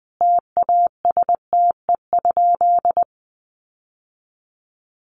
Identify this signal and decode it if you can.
Morse code, heard as a keyed tone, about 20 wpm, 700 Hz. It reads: TASTE?
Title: Sentences from Top 1100 Words 20wpm